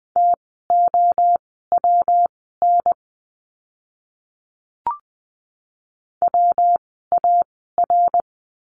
Morse code, heard as a keyed tone, about 20 wpm, spaced as for 15 wpm, 700 Hz. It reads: TOWN WAR
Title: 115 Lesson 19f - Letter w - Words 20wpm